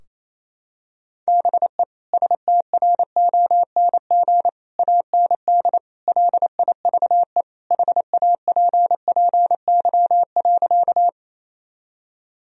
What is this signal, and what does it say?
Morse code, heard as a keyed tone, about 28 wpm, 700 Hz. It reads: BE STRONG AND LIVE HAPPY.
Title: Sentences from Top 1000 Words - Review Part 3 - 28wpm